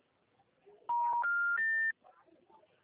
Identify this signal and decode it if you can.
Special information tone.